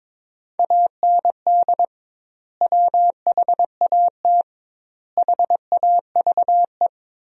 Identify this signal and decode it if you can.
Morse code, heard as a keyed tone, about 22 wpm, 700 Hz. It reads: AND WHAT HAVE
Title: 2x Word-Spacing - Sentences from Top 500 Words - 22wpm